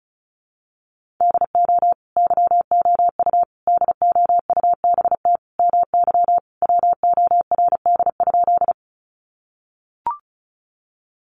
Morse code, heard as a keyed tone, about 34 wpm, 700 Hz. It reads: DO YOU DOUBT MY WORD?
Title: Sentences from Top 1500 Words 35wpm